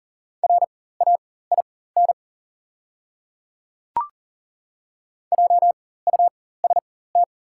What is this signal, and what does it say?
Morse code, heard as a keyed tone, about 40 wpm, spaced as for 19 wpm, 690 Hz. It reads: RAIN JUST
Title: 4x Character Spacing - Top 300 Words - 40wpm